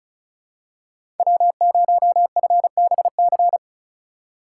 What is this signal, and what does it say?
Morse code, heard as a keyed tone, about 34 wpm, 700 Hz. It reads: W0FBC
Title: USA Call Signs - 5 Characters - 4x Character Spacing - 35wpm